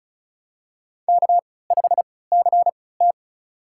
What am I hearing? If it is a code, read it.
Morse code, heard as a keyed tone, about 36 wpm, spaced as for 20 wpm, 700 Hz. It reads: K5CT
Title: USA Call Signs - 3x Character Spacing - 35wpm